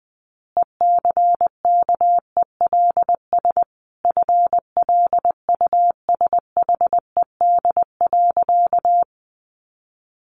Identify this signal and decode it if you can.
Morse code, heard as a keyed tone, about 20 wpm, 700 Hz. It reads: ECKELS FLUSHED.
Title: A Sound Of Thunder 20wpm